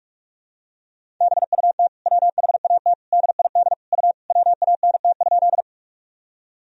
Morse code, heard as a keyed tone, about 45 wpm, 700 Hz. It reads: BUT WHAT DID U WANT?
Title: Sentences from Top 200 Words and Abbreviations 45wpm